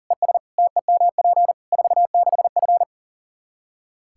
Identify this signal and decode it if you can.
Morse code, heard as a keyed tone, about 40 wpm, 700 Hz. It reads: ES TEMP 46F
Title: Practice QSO - Exchange 2 - Info - 40wpm